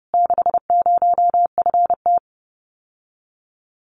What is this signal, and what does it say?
Morse code, heard as a keyed tone, about 30 wpm, 700 Hz. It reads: 60FT
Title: Common QSO Phrases Encore 30wpm